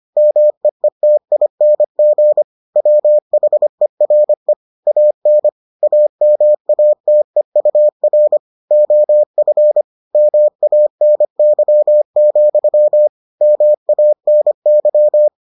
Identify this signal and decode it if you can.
Morse code, heard as a keyed tone, about 25 wpm, 600 Hz. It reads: MEETING WHERE AN AMATEUR OF MANY, MANY